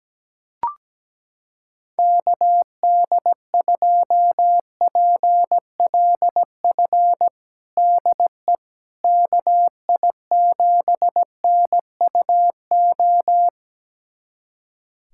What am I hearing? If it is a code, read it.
Morse code, heard as a keyed tone, about 17 wpm, 700 Hz. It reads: KD2PLF DE KI7NUO
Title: USA Callsign DE Callsign - 6 Characters - 17wpm